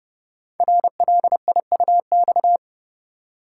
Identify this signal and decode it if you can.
Morse code, heard as a keyed tone, about 30 wpm, 700 Hz. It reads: RLIUX
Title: Code Groups - 5 Letter:Numbers - 30wpm